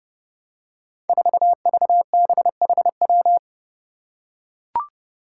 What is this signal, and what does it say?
Morse code, heard as a keyed tone, about 32 wpm, 700 Hz. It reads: 4VBHW
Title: Five Challenging Letter-Number Combinations 30wpm